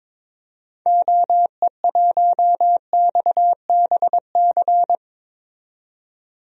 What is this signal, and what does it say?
Morse code, heard as a keyed tone, about 22 wpm, 700 Hz. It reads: OE1XBC
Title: DXCC-100 Call Signs - 6 Characters - 22wpm